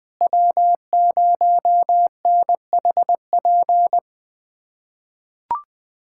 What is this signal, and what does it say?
Morse code, heard as a keyed tone, about 21 wpm, 700 Hz. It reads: W0NHP
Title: USA Callsign DE Callsign - 5 Characters - 20wpm